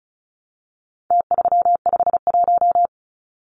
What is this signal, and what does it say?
Morse code, heard as a keyed tone, about 34 wpm, 700 Hz. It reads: T351
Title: Numbers - Contest Style - 4-Digit 35wpm